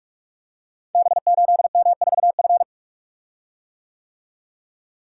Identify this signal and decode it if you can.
Morse code, heard as a keyed tone, about 44 wpm, 690 Hz. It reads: B8M4F
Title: Code Groups - 5 Letter:Numbers - 45wpm